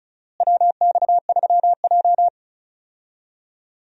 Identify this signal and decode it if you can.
Morse code, heard as a keyed tone, about 36 wpm, 700 Hz. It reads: WX3J
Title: USA Call Signs - 4 Characters - Repeated 2x - 35wpm